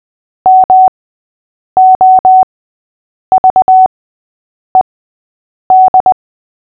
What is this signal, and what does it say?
Morse code, heard as a keyed tone, about 20 wpm, spaced as for 8 wpm, 750 Hz. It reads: MOVED
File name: cw / fists / Short-Story-5-8wpm